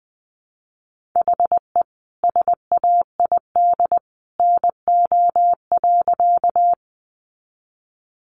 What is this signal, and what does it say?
Morse code, heard as a keyed tone, about 20 wpm, 700 Hz. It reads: HE SAID NO.
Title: Sentences from Top 100 Words 20wpm